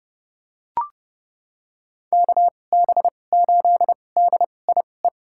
Morse code, heard as a keyed tone, about 30 wpm, spaced as for 22 wpm, 700 Hz. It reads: KB8DIE
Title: USA Call Signs - 6 Characters - 2x Character Spacing - 30wpm